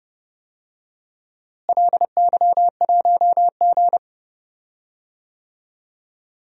Morse code, heard as a keyed tone, about 30 wpm, 700 Hz. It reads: LY1G
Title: DXCC-100 Call Signs - 4 Characters - Repeated 3x - 30wpm